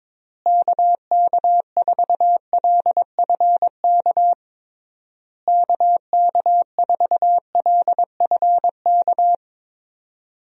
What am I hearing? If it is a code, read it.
Morse code, heard as a keyed tone, about 22 wpm, 700 Hz. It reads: KK4LFK KK4LFK
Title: USA Call Signs - 6 Characters - Repeated 3x - 22wpm